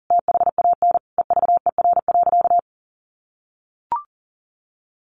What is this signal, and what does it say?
Morse code, heard as a keyed tone, about 40 wpm, 700 Hz. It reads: THAN EVER.
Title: Sentences from Top 1400 Words 40wpm